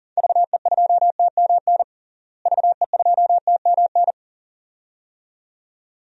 Morse code, heard as a keyed tone, about 40 wpm, 700 Hz. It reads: VE2TMN VE2TMN
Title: DXCC-100 Call Signs - 6 Characters - Repeated 2x - 40wpm